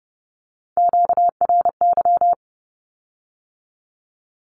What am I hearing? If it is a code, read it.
Morse code, heard as a keyed tone, about 32 wpm, 700 Hz. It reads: QRY
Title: Q-Codes with Spoken Meaning - 30wpm